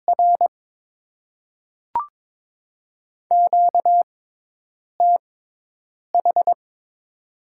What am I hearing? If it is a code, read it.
Morse code, heard as a keyed tone, about 22 wpm, spaced as for 8 wpm, 700 Hz. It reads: R QTH
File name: Top 20 QSO Elements - 6x Character Spacing - 22wpm